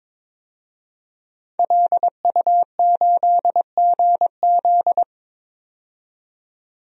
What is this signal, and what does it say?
Morse code, heard as a keyed tone, about 22 wpm, 700 Hz. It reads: LU8GZ
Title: DXCC-100 Call Signs - 5 Characters - 22wpm